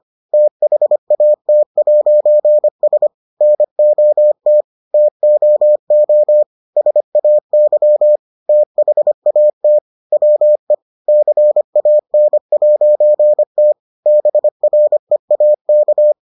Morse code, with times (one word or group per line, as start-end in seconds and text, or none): 0.3-3.1 THAT'S
3.4-4.6 NOT
4.9-6.4 TOO
6.8-8.2 SAY
8.5-9.8 THAT
10.1-10.7 WE
11.1-13.7 CAN'T
14.1-16.1 BREAK